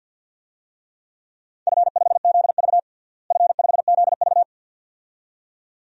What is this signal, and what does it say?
Morse code, heard as a keyed tone, about 50 wpm, 700 Hz. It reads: F5ZV F5ZV
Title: DXCC-100 Call Signs - 4 Characters - Repeated 2x - 50wpm